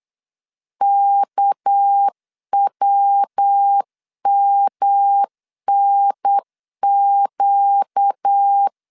Morse code, heard as a keyed tone, about 8 wpm, 800 Hz. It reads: KWMNQ
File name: LEZIONI / 6 PKX0 / 6 PK 8wpm 3003